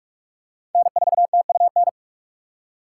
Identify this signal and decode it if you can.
Morse code, heard as a keyed tone, about 43 wpm, 700 Hz. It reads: N4TUN
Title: USA Call Signs - 5 Characters - 6x Character Spacing - 45wpm